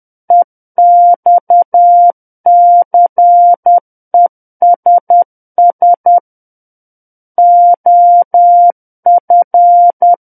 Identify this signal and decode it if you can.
Morse code, heard as a keyed tone, about 10 wpm, 700 Hz. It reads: EXCESS OF
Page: War of the Worlds - 01-Chapter 1 - 10 WPM